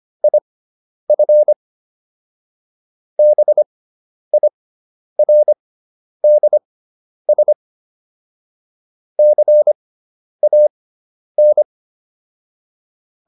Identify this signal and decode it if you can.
Morse code, heard as a keyed tone, about 25 wpm, spaced as for 10 wpm, 600 Hz. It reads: IF BIRDS CAN